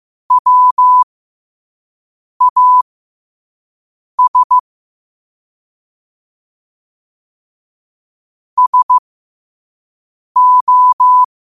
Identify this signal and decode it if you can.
Morse code, heard as a keyed tone, about 15 wpm, spaced as for 5 wpm, 1000 Hz. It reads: WAS SO
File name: qotdfiles / 5wpm